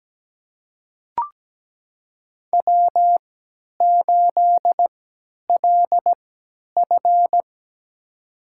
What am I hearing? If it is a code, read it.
Morse code, heard as a keyed tone, about 17 wpm, spaced as for 10 wpm, 700 Hz. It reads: W8LF
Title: USA Call Signs - 4 Characters - 3x Character Spacing - 17wpm